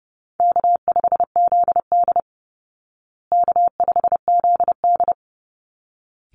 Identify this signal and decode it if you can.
Morse code, heard as a keyed tone, about 28 wpm, 700 Hz. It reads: K5ZD K5ZD
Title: Easy Call Signs Repeated 2x - 30wpm